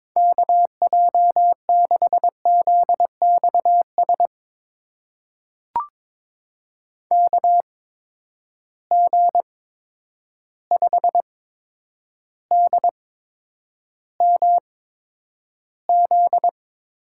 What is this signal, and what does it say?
Morse code, heard as a keyed tone, about 23 wpm, spaced as for 10 wpm, 700 Hz. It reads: KJ6ZXS KG5DMZ
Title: USA Call Signs - 6 Characters - 8x Character Spacing - 22wpm